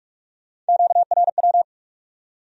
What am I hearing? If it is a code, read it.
Morse code, heard as a keyed tone, about 64 wpm, spaced as for 54 wpm, 700 Hz. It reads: QRW
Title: Speed-Racing Q-Codes - 45wpm